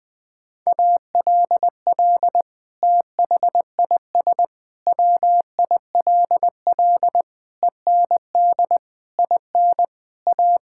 Morse code, heard as a keyed tone, about 20 wpm, 700 Hz. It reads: ALL THIS WILL END IN A
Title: Sentences from Top 1000 Words - Review Part 5 - 20wpm